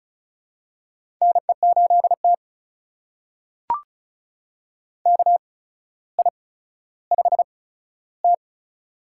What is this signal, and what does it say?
Morse code, heard as a keyed tone, about 36 wpm, spaced as for 15 wpm, 700 Hz. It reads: NE8T KI5T
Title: USA Call Signs - 4 Characters - 8x Character Spacing - 35wpm